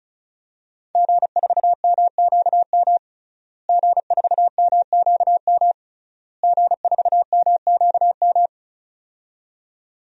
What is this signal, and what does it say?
Morse code, heard as a keyed tone, about 36 wpm, 700 Hz. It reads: G4MQM G4MQM G4MQM
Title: DXCC-100 Call Signs - 5 Characters - Repeated 3x - 35wpm